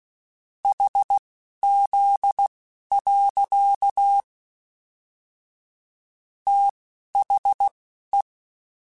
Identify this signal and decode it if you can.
Morse code, heard as a keyed tone, about 16 wpm, spaced as for 11 wpm, 800 Hz. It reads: HZ. THE
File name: Technical Subjects / Technical Subjects-mp3s / Sw Pwr Sup @10